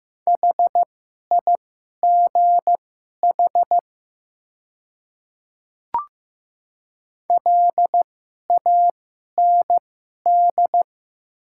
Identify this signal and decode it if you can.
Morse code, heard as a keyed tone, about 15 wpm, spaced as for 11 wpm, 700 Hz. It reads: HIGH LAND
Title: 2x Character Spacing - Top 500 Words - 15wpm